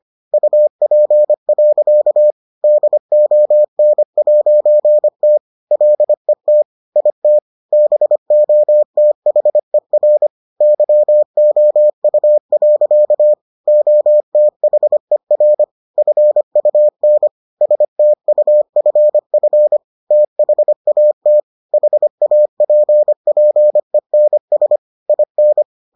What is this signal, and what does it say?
Morse code, heard as a keyed tone, about 25 wpm, 600 Hz. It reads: UP. DON'T LET IT BOTHER YOU. OTHER FUN STUFF THAT HAPPENS IN